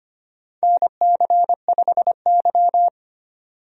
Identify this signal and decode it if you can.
Morse code, heard as a keyed tone, about 25 wpm, 700 Hz. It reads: NC5Y
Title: USA Call Signs - 4 Characters - 25wpm